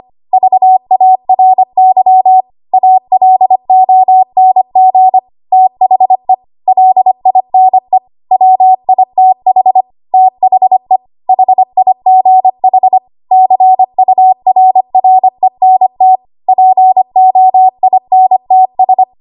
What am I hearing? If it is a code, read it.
Morse code, heard as a keyed tone, about 25 wpm, 750 Hz. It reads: VARY ALONG THE LINE WITH THE HIGH CURRENT POINTS
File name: Archive / 25 WPM / 25WPM